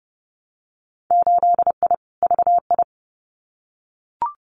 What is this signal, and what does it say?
Morse code, heard as a keyed tone, about 32 wpm, 700 Hz. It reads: 8I VI
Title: Winter Field Day Exchanges Encore 30wpm